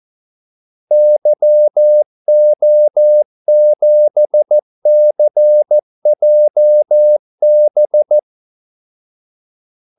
Morse code, heard as a keyed tone, about 14 wpm, 600 Hz. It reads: YO7CJB
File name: Callsigns-14wpm